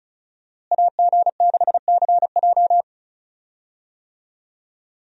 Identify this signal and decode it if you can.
Morse code, heard as a keyed tone, about 34 wpm, 700 Hz. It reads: AG6CJ
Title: USA Call Signs - 5 Characters - Repeated 3x - 35wpm